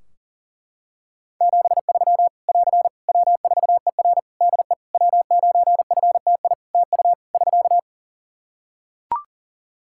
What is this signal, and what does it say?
Morse code, heard as a keyed tone, about 43 wpm, 700 Hz. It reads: 73 <AR> W4ER DE W9FTI TU <SK>
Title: Practice QSO - Exchange 4 - Ending - 40wpm